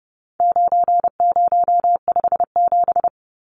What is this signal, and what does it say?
Morse code, heard as a keyed tone, about 30 wpm, 700 Hz. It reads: 9057
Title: Numbers - 4-Digit 30wpm